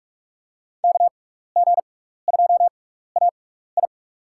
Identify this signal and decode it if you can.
Morse code, heard as a keyed tone, about 43 wpm, spaced as for 15 wpm, 700 Hz. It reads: KG2AI
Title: USA Call Signs - 5 Characters - 6x Character Spacing - 45wpm